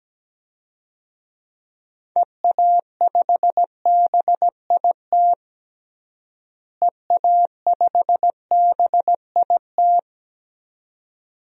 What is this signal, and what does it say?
Morse code, heard as a keyed tone, about 17 wpm, 700 Hz. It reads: EA5BIT EA5BIT
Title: DXCC-100 Call Signs - 6 Characters - Repeated 2x - 17wpm